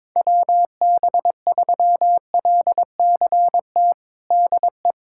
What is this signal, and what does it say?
Morse code, heard as a keyed tone, about 22 wpm, 700 Hz. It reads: WB3LCT DE
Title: USA Callsign DE Callsign 22wpm